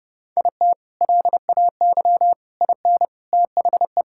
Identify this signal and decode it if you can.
Morse code, heard as a keyed tone, about 32 wpm, 700 Hz. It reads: IT LAY IN THE
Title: Sentences from Top 1000 Words - Review Part 6 - 30wpm